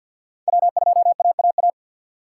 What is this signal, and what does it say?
Morse code, heard as a keyed tone, about 52 wpm, 700 Hz. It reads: W2AAA
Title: USA Call Signs - 3x Character Spacing - 50wpm